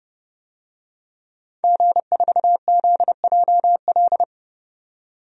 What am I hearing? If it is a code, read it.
Morse code, heard as a keyed tone, about 32 wpm, 700 Hz. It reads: G4ZJL
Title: DXCC-100 Call Signs - 5 Characters - 30wpm